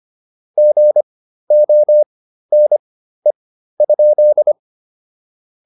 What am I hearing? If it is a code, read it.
Morse code, heard as a keyed tone, about 25 wpm, spaced as for 13 wpm, 600 Hz. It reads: GONE?